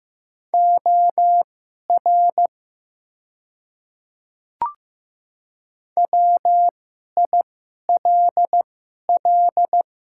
Morse code, heard as a keyed tone, about 15 wpm, spaced as for 11 wpm, 690 Hz. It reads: OR WILL
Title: 2x Character Spacing - Top 500 Words - 15wpm